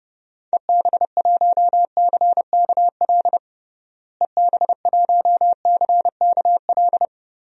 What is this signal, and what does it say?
Morse code, heard as a keyed tone, about 30 wpm, 700 Hz. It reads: EB1CKL EB1CKL
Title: DXCC-100 Call Signs - 6 Characters - Repeated 2x - 30wpm